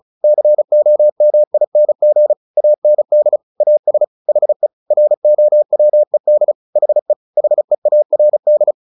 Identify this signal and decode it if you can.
Morse code, heard as a keyed tone, about 35 wpm, 600 Hz. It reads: COMING AND AS HE ROWED HE HEARD
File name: the old man and the sea / the old man and the sea 1 35wpm